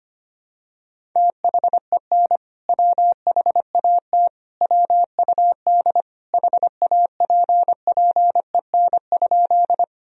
Morse code, heard as a keyed tone, about 25 wpm, 700 Hz. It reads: THEN WHAT WUD HAPPEN?
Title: Sentences from Top 400 Words and Abbreviations 25wpm